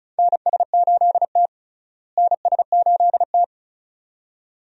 Morse code, heard as a keyed tone, about 36 wpm, 700 Hz. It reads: NS8T NS8T
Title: USA Call Signs - 4 Characters - Repeated 2x - 35wpm